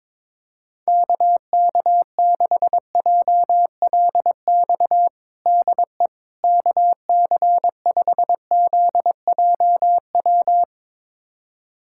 Morse code, heard as a keyed tone, about 22 wpm, 700 Hz. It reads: KK6JLX DE KC5ZJW
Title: USA Callsign DE Callsign 22wpm